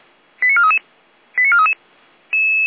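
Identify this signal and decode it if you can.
Five-tone selective call (ZVEI), heard as quick runs of five tones, tones of about 70 ms. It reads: 89433 89433
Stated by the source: ZVEI five-tone selective call received on SDR.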